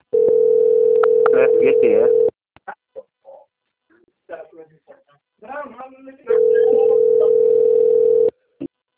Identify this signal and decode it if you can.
Ringback tone (North America), 440 + 480 Hz, with speech.
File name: out- - - - - .17316